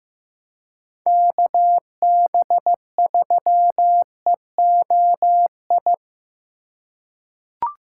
Morse code, heard as a keyed tone, about 15 wpm, 700 Hz. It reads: KB3EOI
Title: USA Call Signs - 6 Characters - 8x Character Spacing - 15wpm